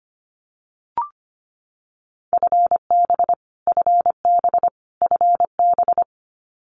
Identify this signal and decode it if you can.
Morse code, heard as a keyed tone, about 25 wpm, 700 Hz. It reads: FB FB FB
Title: Abbreviations Repeated 3x - 25wpm